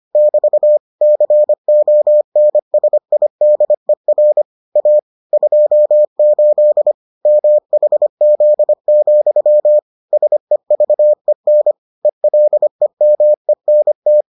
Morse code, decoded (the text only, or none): <BT> CONSIDER A 28 MHZ, SEVEN ELEMENT